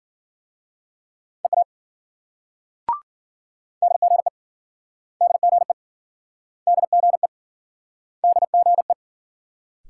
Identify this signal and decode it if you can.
Morse code, heard as a keyed tone, about 53 wpm, 700 Hz. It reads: EA DGE DGE DGE DGE